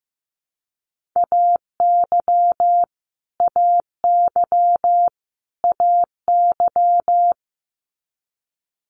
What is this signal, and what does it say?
Morse code, heard as a keyed tone, about 15 wpm, 700 Hz. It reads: AY AY AY
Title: Multi-letter Phonemes Repeated 3x 15wpm